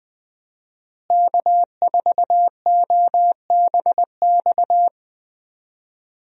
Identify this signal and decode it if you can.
Morse code, heard as a keyed tone, about 20 wpm, 700 Hz. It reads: K4OBX
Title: USA Call Signs - 5 Characters - Repeated 2x - 20wpm